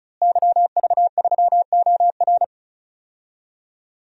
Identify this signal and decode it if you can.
Morse code, heard as a keyed tone, about 36 wpm, 700 Hz. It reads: YV3OR
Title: DXCC-100 Call Signs - 5 Characters - Repeated 3x - 35wpm